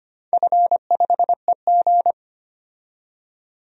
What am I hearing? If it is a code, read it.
Morse code, heard as a keyed tone, about 25 wpm, 700 Hz. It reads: F5EG
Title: DXCC-100 Call Signs - 4 Characters - 25wpm